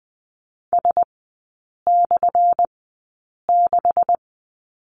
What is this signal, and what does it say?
Morse code, heard as a keyed tone, about 20 wpm, spaced as for 8 wpm, 700 Hz. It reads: S/6
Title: 178 Lesson 29e - Slash - Three Characters 20wpm